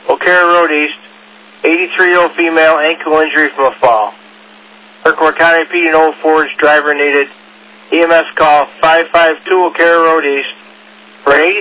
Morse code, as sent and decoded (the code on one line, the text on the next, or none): none